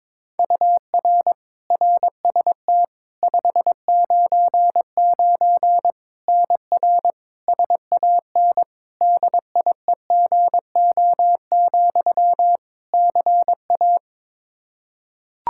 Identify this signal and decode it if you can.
Morse code, heard as a keyed tone, about 22 wpm, 700 Hz. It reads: UR RST 599 NR SAN DIEGO, CA
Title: Practice QSO - Exchange 1 - Beginning - 22wpm